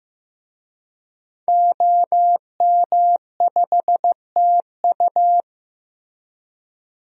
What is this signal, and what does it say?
Morse code, heard as a keyed tone, about 15 wpm, 700 Hz. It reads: OM5TU
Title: DXCC-100 Call Signs - 5 Characters - Repeated 3x - 15wpm